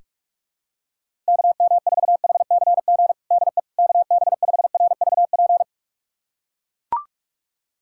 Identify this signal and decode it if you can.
Morse code, heard as a keyed tone, about 42 wpm, 700 Hz. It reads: KM4HCG DE KB5RVP
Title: USA Callsign DE Callsign 45wpm